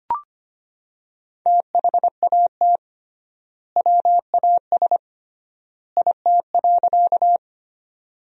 Morse code, heard as a keyed tone, about 25 wpm, 700 Hz. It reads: THAT WAS IT.
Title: 3x Word-Spacing - Sentences from Top 100 Words - 25wpm